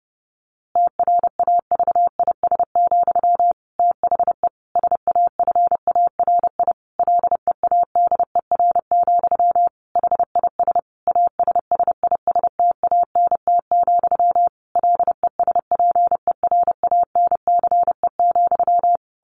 Morse code, heard as a keyed tone, about 30 wpm, 700 Hz. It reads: TRAVIS, THE SAFARI LEADER, HIS ASSISTANT, LESPERANCE,
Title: A Sound Of Thunder 30wpm